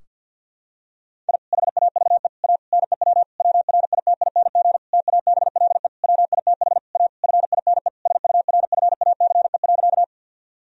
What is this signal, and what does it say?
Morse code, heard as a keyed tone, about 51 wpm, 700 Hz. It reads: I HAVE A NEW WRITING TABLE WITH A FINE SURFACE.
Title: Sentences from Top 1000 Words - Review Part 3 - 50wpm